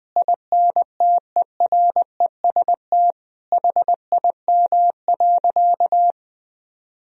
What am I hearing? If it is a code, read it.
Morse code, heard as a keyed tone, about 20 wpm, 700 Hz. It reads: INTEREST HIM.
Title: Sentences from Top 1000 Words - Review Part 2 - 20wpm